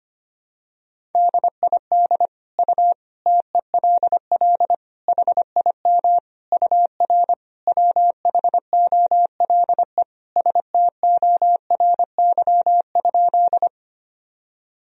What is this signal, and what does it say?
Morse code, heard as a keyed tone, about 25 wpm, 700 Hz. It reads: DID U TELL HIM UR WHOLE STORY?
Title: Sentences from Top 400 Words and Abbreviations 25wpm